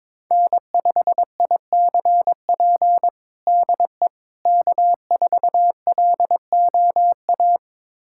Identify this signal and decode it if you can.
Morse code, heard as a keyed tone, about 22 wpm, 700 Hz. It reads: N5ICP DE K4LOA
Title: USA Callsign DE Callsign - 5 Characters - 22wpm